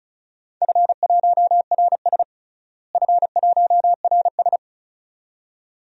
Morse code, heard as a keyed tone, about 36 wpm, 690 Hz. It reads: F1RS F1RS
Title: DXCC-100 Call Signs - 4 Characters - Repeated 2x - 35wpm